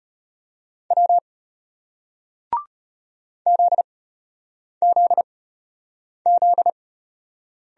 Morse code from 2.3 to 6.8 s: ZZZ

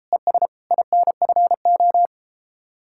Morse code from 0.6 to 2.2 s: INFO

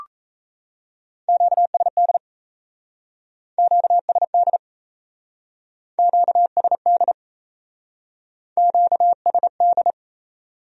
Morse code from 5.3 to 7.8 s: QSD

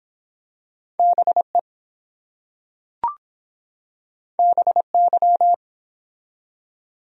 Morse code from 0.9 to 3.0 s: BE

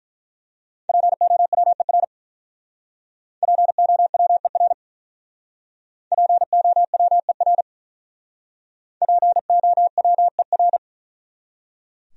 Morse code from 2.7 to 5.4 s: POWER

